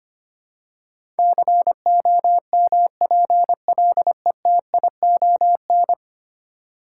Morse code, heard as a keyed tone, about 25 wpm, 700 Hz. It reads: COMPLETION
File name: Ten Letter Words - 25wpm